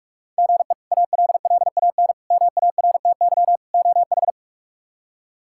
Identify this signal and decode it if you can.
Morse code, heard as a keyed tone, about 42 wpm, 700 Hz. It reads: GE ALLAN MARTY OH
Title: K1USN SST QSO Practice - 45wpm